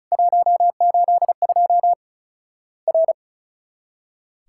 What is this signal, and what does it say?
Morse code, heard as a keyed tone, about 36 wpm, 700 Hz. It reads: 182 R
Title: Worked All Europe DX Contest - QTC Exchange - 35wpm